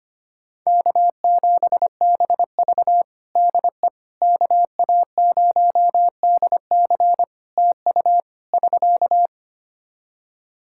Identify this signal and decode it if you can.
Morse code, heard as a keyed tone, about 25 wpm, 700 Hz. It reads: K7BV DE KA0DC TU <SK>
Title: Practice QSO - Exchange 4 - Ending - 25wpm